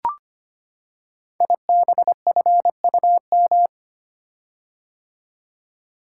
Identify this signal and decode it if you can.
Morse code, heard as a keyed tone, about 25 wpm, 700 Hz. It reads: IBFUM
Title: Code Groups - 5 Letter:Numbers - 25wpm